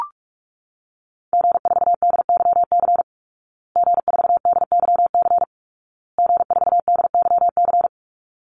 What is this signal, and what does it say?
Morse code, heard as a keyed tone, about 44 wpm, 700 Hz. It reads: G4DYC G4DYC G4DYC
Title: Call Signs Repeated 2x - 45wpm